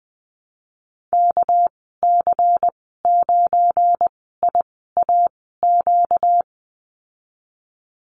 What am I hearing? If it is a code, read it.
Morse code, heard as a keyed tone, about 20 wpm, spaced as for 14 wpm, 700 Hz. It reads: KC9IAQ
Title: 245 Lesson 39g - Letter z - Call Signs 20wpm